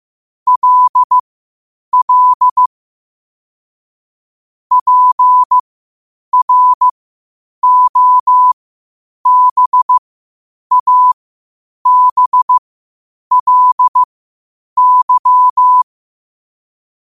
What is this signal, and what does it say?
Morse code, heard as a keyed tone, about 15 wpm, spaced as for 8 wpm, 1000 Hz. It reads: LL PROBABLY
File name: qotdfiles / 8wpm